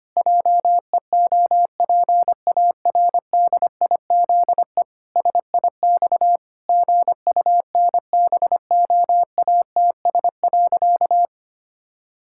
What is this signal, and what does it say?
Morse code, heard as a keyed tone, about 25 wpm, 700 Hz. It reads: JEOPARDIZE SIX GUNBOATS.
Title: Pangram Challenge v2 - 25wpm